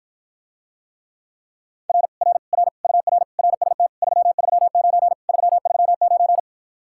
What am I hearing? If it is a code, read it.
Morse code, heard as a keyed tone, about 54 wpm, 700 Hz. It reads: R R R UR RST 339 339